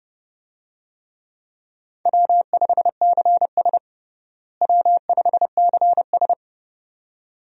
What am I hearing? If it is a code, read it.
Morse code, heard as a keyed tone, about 29 wpm, 700 Hz. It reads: W5CS W5CS
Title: USA Call Signs - 4 Characters - Repeated 2x - 30wpm